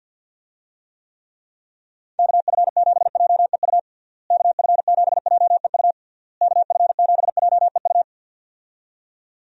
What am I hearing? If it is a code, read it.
Morse code, heard as a keyed tone, about 52 wpm, 700 Hz. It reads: KF7JEU KF7JEU KF7JEU
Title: USA Call Signs - 6 Characters - Repeated 3x - 50wpm